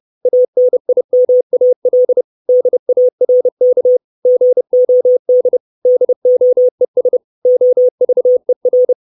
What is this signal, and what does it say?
Morse code, heard as a keyed tone, about 30 wpm, 500 Hz. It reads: ANIMAL DARK GOD DOES OVER
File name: 632 Words 30WPM